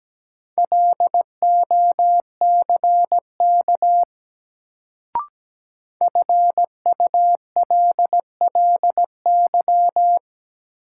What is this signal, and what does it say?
Morse code, heard as a keyed tone, about 17 wpm, 700 Hz. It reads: LOCK FULLY
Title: Sentences from Top 1300 Words - New Word Drill - 17wpm